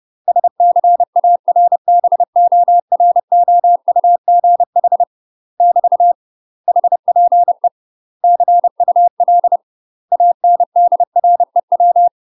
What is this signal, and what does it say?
Morse code, heard as a keyed tone, about 30 wpm, 700 Hz. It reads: SCARBOROUGH <BT> HPE CUL ANDREW
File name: UK MASTER QSOs TRAINING/30WPM